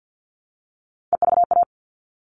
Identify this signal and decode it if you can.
Morse code, heard as a keyed tone, about 50 wpm, 700 Hz. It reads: EVA
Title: All Cut Numbers - 3-Digit 50wpm